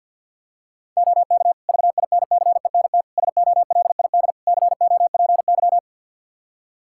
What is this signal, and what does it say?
Morse code, heard as a keyed tone, about 47 wpm, 700 Hz. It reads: OK VINCENT SOLID COPY
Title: Practice QSO - Exchange 3 - More Info - 50wpm